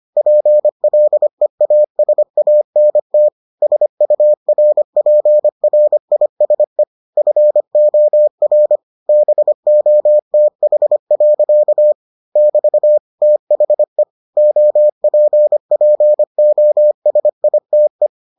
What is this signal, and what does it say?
Morse code, heard as a keyed tone, about 25 wpm, 600 Hz. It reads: PLEASANT SURPRISE FOR BOTH. <BT> THE OPPOSITE